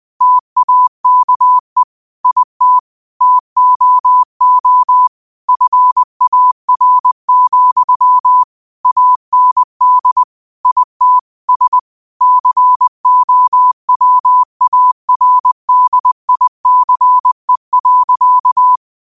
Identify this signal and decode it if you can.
Morse code, heard as a keyed tone, about 20 wpm, 1000 Hz. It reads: TAKE IT TOO FAR, AND IT S COWARDICE.